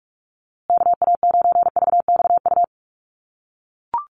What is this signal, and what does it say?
Morse code, heard as a keyed tone, about 44 wpm, 700 Hz. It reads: KA9VXU
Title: USA Call Signs Repeated 2x 45wpm